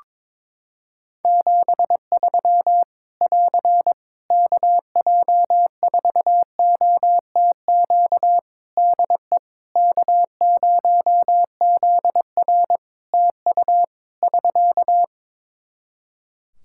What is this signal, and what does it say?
Morse code, heard as a keyed tone, about 21 wpm, 700 Hz. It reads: 73 <AR> KJ4OTQ DE K0ZR TU <SK>
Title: Practice QSO - Exchange 4 - Ending - 22wpm